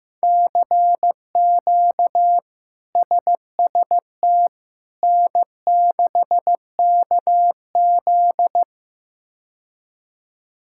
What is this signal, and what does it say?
Morse code, heard as a keyed tone, about 15 wpm, 700 Hz. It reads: CQ SST N6KZ
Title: K1USN SST QSO Practice - 15wpm